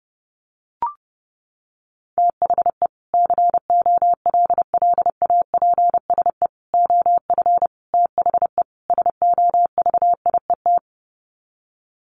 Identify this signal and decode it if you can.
Morse code, heard as a keyed tone, about 30 wpm, 700 Hz. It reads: THE COLLAPSE OF THE SOVIET
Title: Sets of 5 Words Encore 30wpm